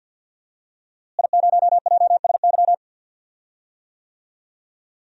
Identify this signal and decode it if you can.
Morse code, heard as a keyed tone, about 51 wpm, 700 Hz. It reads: I0JSY
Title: DXCC-100 Call Signs - 5 Characters - Repeated 3x - 50wpm